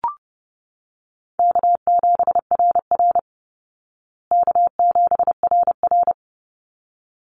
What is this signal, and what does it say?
Morse code, heard as a keyed tone, about 30 wpm, 700 Hz. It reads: K7RR K7RR
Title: Easy Call Signs Repeated 2x - 30wpm